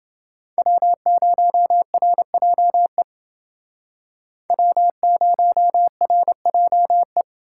Morse code, heard as a keyed tone, about 28 wpm, 700 Hz. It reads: W0RJE W0RJE